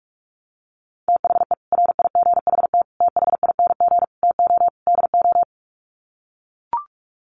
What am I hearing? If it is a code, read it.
Morse code, heard as a keyed tone, about 44 wpm, 700 Hz. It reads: THE RIGHT THING TO DO
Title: Sets of 5 Words 45wpm